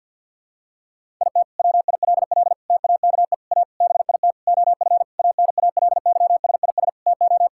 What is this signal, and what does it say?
Morse code, heard as a keyed tone, about 48 wpm, 700 Hz. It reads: IT WILL TAKE A BIT OF ANALYSIS TO
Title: Sentences from Top 1000 Words - Review Part 5 - 50wpm